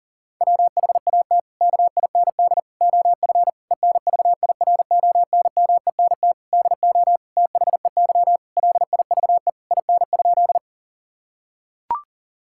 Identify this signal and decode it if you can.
Morse code, heard as a keyed tone, about 40 wpm, 700 Hz. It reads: WHAT KIND OF ENVIRONMENT DO THEY LIVE IN?
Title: Sentences from Top 1000 Words - Review Part 5 - 40wpm